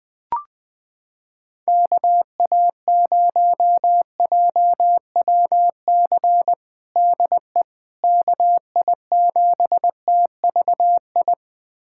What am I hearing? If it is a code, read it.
Morse code, heard as a keyed tone, about 19 wpm, 700 Hz. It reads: KA0JWC DE KI7TVI
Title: USA Callsign DE Callsign - 6 Characters - 20wpm